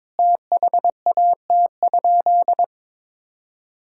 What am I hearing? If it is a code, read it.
Morse code, heard as a keyed tone, about 22 wpm, 700 Hz. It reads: THAT?
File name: practice / 22wpm / Sentences with 4-Word Phrases - 22wpm